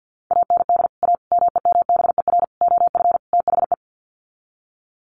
Morse code, heard as a keyed tone, about 51 wpm, 700 Hz. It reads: AND A MEMBER OF THE
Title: Sets of 5 Words 50wpm